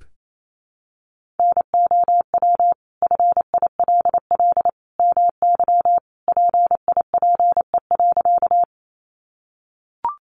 Morse code, heard as a keyed tone, about 28 wpm, 700 Hz. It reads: NOW FILL MY PIPE.
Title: Sentences from Top 1500 Words 28wpm